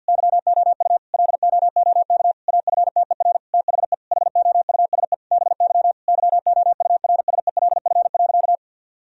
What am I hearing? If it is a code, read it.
Morse code, heard as a keyed tone, about 48 wpm, 700 Hz. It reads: YOU LOOK AFTER THE HOUSE BY YOURSELF.
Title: Sentences from Top 1000 Words - Review Part 4 - 50wpm